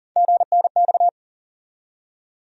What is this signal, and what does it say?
Morse code, heard as a keyed tone, about 37 wpm, 700 Hz. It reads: GNX
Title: USA Call Sign Postfixes - 3 Letters 40wpm